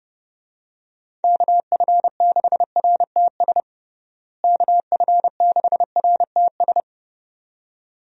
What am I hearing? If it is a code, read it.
Morse code, heard as a keyed tone, about 29 wpm, 700 Hz. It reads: KF6RTS KF6RTS
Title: USA Call Signs - 6 Characters - Repeated 2x - 30wpm